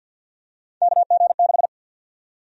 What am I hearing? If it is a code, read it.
Morse code, heard as a keyed tone, about 53 wpm, 700 Hz. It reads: KG6
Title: USA Call Sign Prefixes - 50wpm